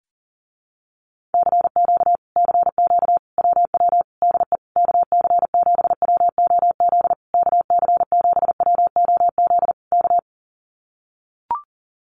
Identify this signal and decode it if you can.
Morse code, heard as a keyed tone, about 40 wpm, 700 Hz. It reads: CQ CQ WW DE KC7WOZ KC7WOZ K
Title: Common QSO Phrases Encore 40wpm